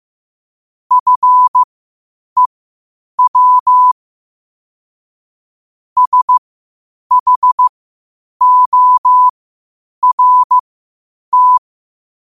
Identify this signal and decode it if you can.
Morse code, heard as a keyed tone, about 15 wpm, spaced as for 8 wpm, 1000 Hz. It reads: FEW SHORT
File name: qotdfiles / 8wpm